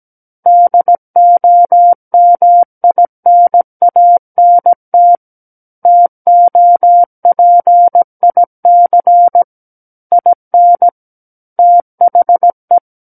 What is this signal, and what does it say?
Morse code, heard as a keyed tone, about 17 wpm, 700 Hz. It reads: DOMINANT TOPIC IN THE
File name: Book One / 08-Chapter 8 - 17 WPM